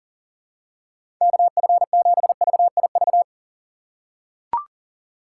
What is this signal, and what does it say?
Morse code, heard as a keyed tone, about 37 wpm, 700 Hz. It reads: KF7VIV
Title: USA Call Signs - 6 Characters - 4x Character Spacing - 40wpm